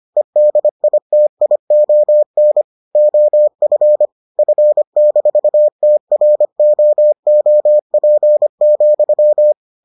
Morse code, heard as a keyed tone, about 25 wpm, 600 Hz. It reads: EDITION OF F-TROOP,